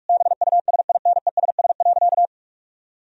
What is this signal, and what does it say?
Morse code, heard as a keyed tone, about 46 wpm, 700 Hz. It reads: BUSINESS.
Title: Sentences from Top 1000 Words - Review Part 4 - 45wpm